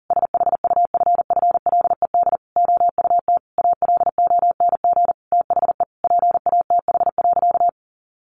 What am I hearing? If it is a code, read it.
Morse code, heard as a keyed tone, about 40 wpm, 700 Hz. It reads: SHUFFLED OUT ALONG THE PATH.
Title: A Sound Of Thunder 40wpm